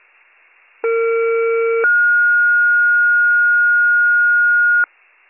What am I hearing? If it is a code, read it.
Two-tone page, heard as a one-second tone, then a three-second tone, about 460 then 1430 Hz. It is heard from 0.8 to 4.8 s.